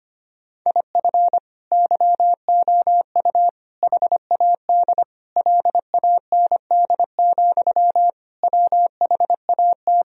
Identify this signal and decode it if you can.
Morse code, heard as a keyed tone, about 25 wpm, 700 Hz. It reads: IF YOU HAD LAND, WHAT
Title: Sentences from Top 1000 Words - Review Part 1 - 25wpm